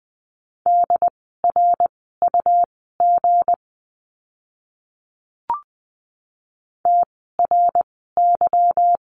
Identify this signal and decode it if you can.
Morse code, heard as a keyed tone, about 20 wpm, spaced as for 15 wpm, 700 Hz. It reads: DRUG TRY
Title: 160 Lesson 26f - Letter g - Words 20wpm